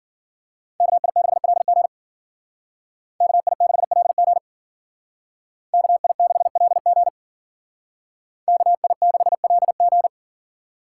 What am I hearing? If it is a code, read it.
Morse code, heard as a keyed tone, about 54 wpm, 700 Hz. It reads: KI6LG KI6LG KI6LG KI6LG